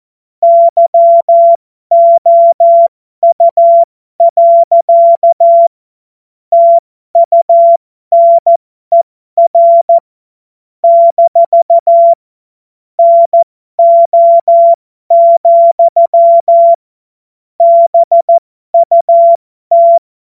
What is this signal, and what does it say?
Morse code, heard as a keyed tone, about 14 wpm, spaced as for 12 wpm, 680 Hz. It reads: YOU. TUNER - NO, BUT
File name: Morsekurs / Witze